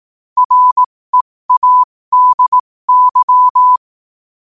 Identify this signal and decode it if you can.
Morse code, heard as a keyed tone, about 18 wpm, spaced as for 15 wpm, 1000 Hz. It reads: READY
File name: qotdfiles / 15wpm